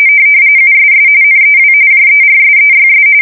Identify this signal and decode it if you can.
RTTY, about 45.45 baud. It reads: USS NEW JERSEYS
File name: fsk170 / USS-New-Jersey